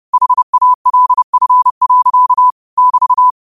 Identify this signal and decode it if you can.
Morse code, heard as a keyed tone, about 30 wpm, 1000 Hz. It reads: HALF. <BT>
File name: qotdfiles / 30wpm